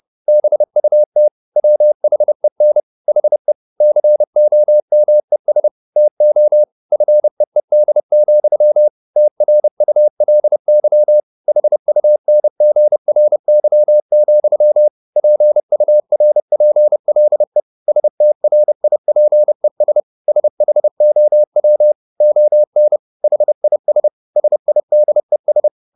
Morse code, heard as a keyed tone, about 30 wpm, 600 Hz. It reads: BUT WHEN HE COMES TO FEED, TRULY HUNGRY, PURPLE STRIPES SHOW ON HIS SIDES